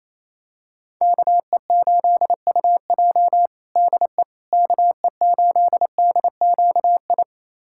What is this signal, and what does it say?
Morse code, heard as a keyed tone, about 28 wpm, 700 Hz. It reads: KE8UJ DE KE8DQI
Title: USA Callsign DE Callsign 28wpm